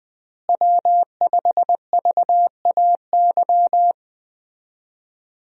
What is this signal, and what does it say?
Morse code, heard as a keyed tone, about 20 wpm, 700 Hz. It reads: W5VAY
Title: USA Call Signs - 5 Characters - Repeated 2x - 20wpm